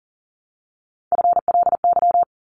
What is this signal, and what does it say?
Morse code, heard as a keyed tone, about 40 wpm, 700 Hz. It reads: FLY
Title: Top 500 Words 40wpm